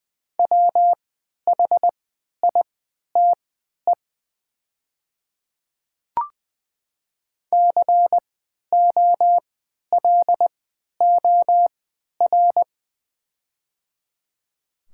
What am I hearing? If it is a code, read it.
Morse code, heard as a keyed tone, about 20 wpm, spaced as for 11 wpm, 700 Hz. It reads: WHITE COLOR
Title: 3x Character Spacing - Top 500 Words - 20wpm